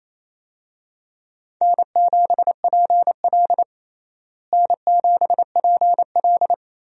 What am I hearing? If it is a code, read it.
Morse code, heard as a keyed tone, about 28 wpm, 700 Hz. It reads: N7PL N7PL
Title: USA Call Signs - 4 Characters - Repeated 3x - 28wpm